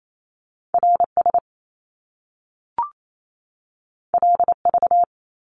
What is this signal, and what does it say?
Morse code, heard as a keyed tone, about 28 wpm, 700 Hz. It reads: R S L V
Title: Top Two Letter Combinations in Words 28wpm